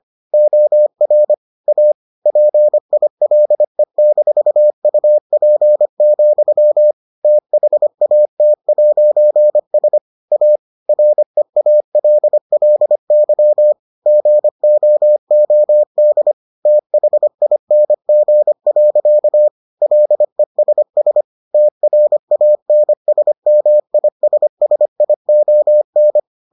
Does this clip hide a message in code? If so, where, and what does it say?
Yes: Morse code from 0.3 to 26.2 s, heard as a keyed tone, about 25 wpm, 600 Hz. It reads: OR A PILE-UP, THAT'S A REALLY GOOD THING. LESS TRANSMISSION